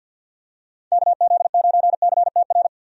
Morse code, heard as a keyed tone, about 46 wpm, 700 Hz. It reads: KZ9CTR
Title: USA Call Signs - 6 Characters - 50wpm